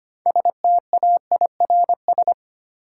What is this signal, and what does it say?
Morse code, heard as a keyed tone, about 25 wpm, 700 Hz. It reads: STAIRS
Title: Sets of 5 Words Encore - New Word Drill - 25wpm